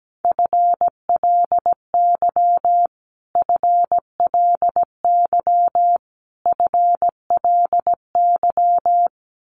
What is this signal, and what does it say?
Morse code, heard as a keyed tone, about 17 wpm, 700 Hz. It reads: FLY FLY FLY
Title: Top 500 Words Repeated 17wpm